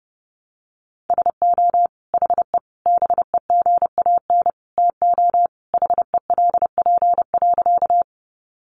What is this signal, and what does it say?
Morse code, heard as a keyed tone, about 28 wpm, 700 Hz. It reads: SO HE BEGAN TO HELP.
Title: Sentences from Top 300 Words 30wpm